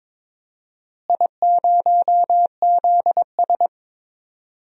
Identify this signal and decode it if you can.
Morse code, heard as a keyed tone, about 22 wpm, 700 Hz. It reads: I0ZS
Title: DXCC-100 Call Signs - 4 Characters - Repeated 3x - 22wpm